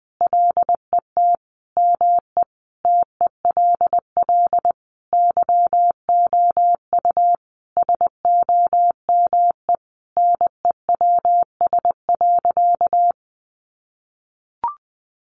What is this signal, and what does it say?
Morse code, heard as a keyed tone, about 20 wpm, 700 Hz. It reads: LET ME TELL YOU SOME NEWS.
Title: Sentences from Top 800 Words 20wpm